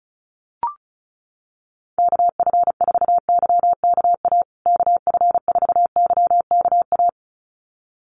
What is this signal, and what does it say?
Morse code, heard as a keyed tone, about 34 wpm, 700 Hz. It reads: KF4YKA KF4YKA
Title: USA Call Signs Repeated 2x 35wpm